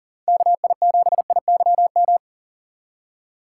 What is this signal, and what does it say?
Morse code, heard as a keyed tone, about 49 wpm, 700 Hz. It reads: KI7IYM
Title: Speed Racing - USA Call Signs 40wpm